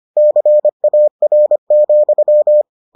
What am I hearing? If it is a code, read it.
Morse code, heard as a keyed tone, about 25 wpm, 600 Hz. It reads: CAR,